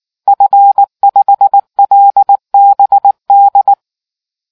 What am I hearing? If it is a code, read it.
Morse code, heard as a keyed tone, about 19 wpm, 800 Hz. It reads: F5LBD